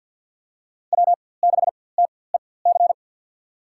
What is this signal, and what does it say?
Morse code, heard as a keyed tone, about 53 wpm, spaced as for 24 wpm, 700 Hz. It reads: W6TEC
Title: USA Call Signs - 5 Characters - 4x Character Spacing - 50wpm